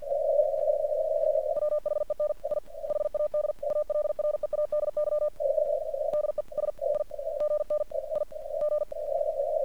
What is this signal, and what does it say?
Morse code, heard as a keyed tone, about 48 wpm, spaced as for 36 wpm, 1200 Hz, with noise. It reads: WHEN I HAD ALREADY DESIGNIG
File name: archived / CRSnet